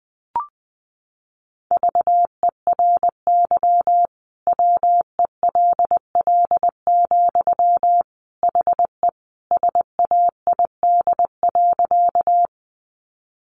Morse code, heard as a keyed tone, about 20 wpm, 700 Hz. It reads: VERY WELL, HE SAID.
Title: Sentences from Top 200 Words 20wpm